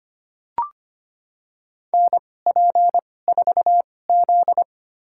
Morse code, heard as a keyed tone, about 25 wpm, spaced as for 18 wpm, 700 Hz. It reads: NP4Z
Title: USA Call Signs - 4 Characters - 2x Character Spacing - 25wpm